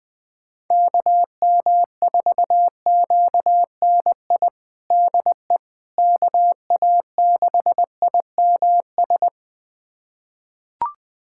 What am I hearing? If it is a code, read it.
Morse code, heard as a keyed tone, about 21 wpm, 700 Hz. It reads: KM4QNI DE KA6IMS
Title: USA Callsign DE Callsign - 6 Characters - 20wpm